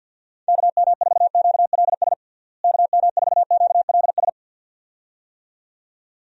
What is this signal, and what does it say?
Morse code, heard as a keyed tone, about 51 wpm, 700 Hz. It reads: KM4QLS KM4QLS
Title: USA Call Signs - 6 Characters - Repeated 2x - 50wpm